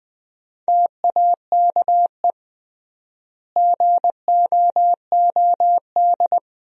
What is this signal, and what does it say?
Morse code, heard as a keyed tone, about 21 wpm, 700 Hz. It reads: TAKE GOOD
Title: 3x Word-Spacing - Sentences from Top 400 Words - 20wpm